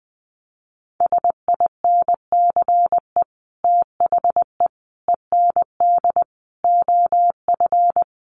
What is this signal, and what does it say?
Morse code, heard as a keyed tone, about 19 wpm, 700 Hz. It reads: SINCE THE END OF
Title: Sets of 5 Words Encore 20wpm